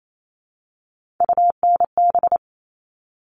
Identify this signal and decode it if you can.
Morse code, heard as a keyed tone, about 28 wpm, 700 Hz. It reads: UNB
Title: All Cut Numbers - 3-Digit 28wpm